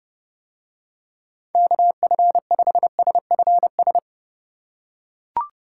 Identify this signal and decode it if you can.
Morse code, heard as a keyed tone, about 31 wpm, 700 Hz. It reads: KF5SFS